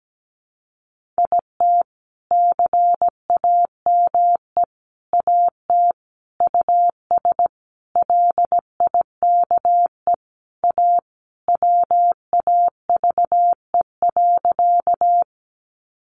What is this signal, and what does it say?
Morse code, heard as a keyed tone, about 17 wpm, 700 Hz. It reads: IT CAME AT US LIKE A WAVE.
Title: Sentences from Top 1400 Words 17wpm